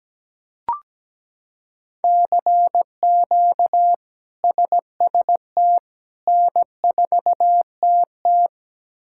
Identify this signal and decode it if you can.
Morse code, heard as a keyed tone, about 17 wpm, 700 Hz. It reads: CQ SST N4TT
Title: K1USN SST QSO Practice - 17wpm